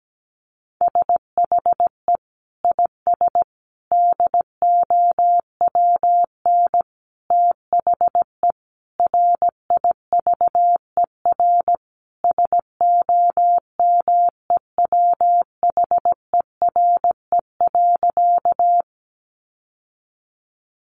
Morse code, heard as a keyed tone, about 17 wpm, 700 Hz. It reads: SHE IS DOWN THE RIVER SOMEWHERE.
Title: Sentences from Top 1400 Words 17wpm